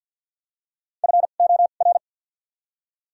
Morse code, heard as a keyed tone, about 53 wpm, spaced as for 37 wpm, 700 Hz. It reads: FOR
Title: 2x Character Spacing - Top 200 Words - 50wpm